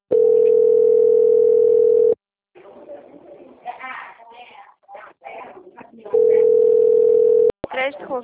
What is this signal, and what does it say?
Ringback tone (North America), 440 + 480 Hz, with speech.
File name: out- - - - - .16735